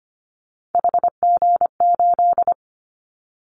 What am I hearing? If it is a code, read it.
Morse code, heard as a keyed tone, about 25 wpm, 700 Hz. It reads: HG8
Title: Callsign Prefixes for Top 25 DXCC - 25wpm